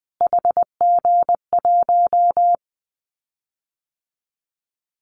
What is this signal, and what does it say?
Morse code, heard as a keyed tone, about 20 wpm, 700 Hz. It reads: HG1
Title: Callsign Prefixes for Top 25 DXCC - Repeated 3x - 20wpm